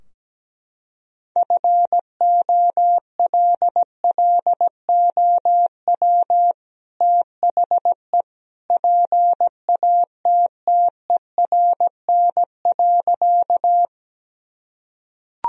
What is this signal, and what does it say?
Morse code, heard as a keyed tone, about 17 wpm, 700 Hz. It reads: FOLLOW THE PATTERN.
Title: Sentences from Top 1000 Words - Review Part 2 - 17wpm